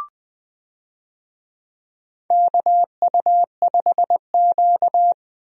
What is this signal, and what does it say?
Morse code, heard as a keyed tone, about 20 wpm, 700 Hz. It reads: KU5Q
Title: USA Call Signs - 4 Characters - Repeated 3x - 20wpm